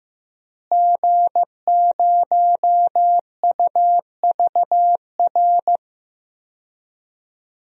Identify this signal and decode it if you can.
Morse code, heard as a keyed tone, about 16 wpm, 700 Hz. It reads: G0UVR
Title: DXCC-100 Call Signs - 5 Characters - Repeated 3x - 15wpm